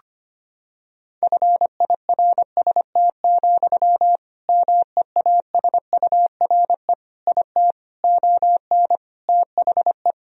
Morse code, heard as a keyed tone, about 25 wpm, 700 Hz. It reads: FIRST, MEASURE IT ON THE
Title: Sentences from Top 1000 Words - Review Part 3 - 25wpm